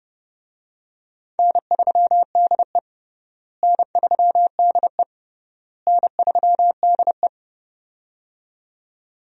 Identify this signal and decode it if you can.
Morse code, heard as a keyed tone, about 31 wpm, 700 Hz. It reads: N3DE N3DE N3DE
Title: USA Call Signs - 4 Characters - Repeated 3x - 30wpm